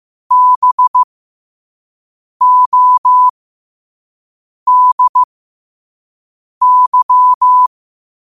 Morse code, heard as a keyed tone, about 15 wpm, spaced as for 5 wpm, 1000 Hz. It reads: BODY
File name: qotdfiles / 5wpm